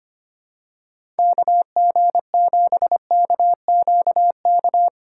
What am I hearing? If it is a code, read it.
Morse code, heard as a keyed tone, about 25 wpm, 700 Hz. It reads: KG7KQK
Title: USA Call Signs - 8x Character Spacing - 25wpm